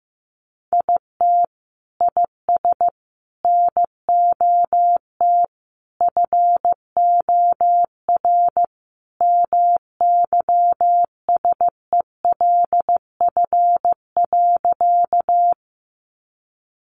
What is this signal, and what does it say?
Morse code, heard as a keyed tone, about 15 wpm, 700 Hz. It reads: IT IS NOT FOR MYSELF.
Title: Sentences from Top 1300 Words 15wpm